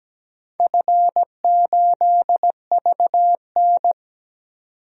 Morse code, heard as a keyed tone, about 17 wpm, 700 Hz. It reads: F8VN
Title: DXCC-100 Call Signs - 4 Characters - 17wpm